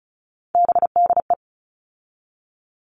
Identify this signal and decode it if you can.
Morse code, heard as a keyed tone, about 34 wpm, 700 Hz. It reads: BDE
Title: All Cut Numbers - 3-Digit 35wpm